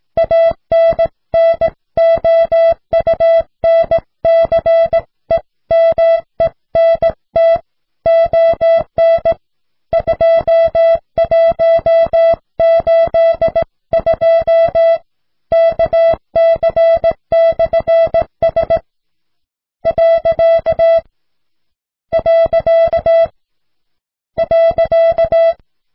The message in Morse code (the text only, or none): ANNOUNCEMENT ON 2182 KC/S . .